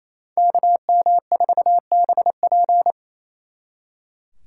Morse code, heard as a keyed tone, about 34 wpm, 700 Hz. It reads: KM4BP
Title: Speed Racing - USA 2x2 Call Signs 28wpm